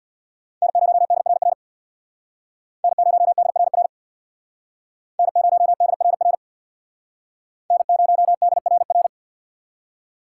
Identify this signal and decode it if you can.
Morse code, heard as a keyed tone, about 62 wpm, 700 Hz. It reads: N0DRR N0DRR N0DRR N0DRR
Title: Speed Racing - USA 1x3 Call Signs 50wpm